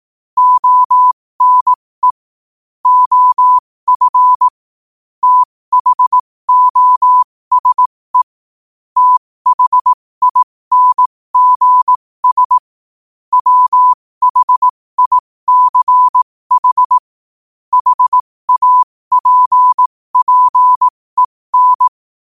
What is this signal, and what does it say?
Morse code, heard as a keyed tone, about 18 wpm, spaced as for 15 wpm, 1000 Hz. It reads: ONE OF THOSE THINGS WHICH HAPPEN
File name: qotdfiles / 15wpm